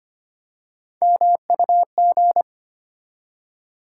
Morse code, heard as a keyed tone, about 25 wpm, 700 Hz. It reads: MUG
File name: practice / 25wpm / Easy Words with Hints - 25wpm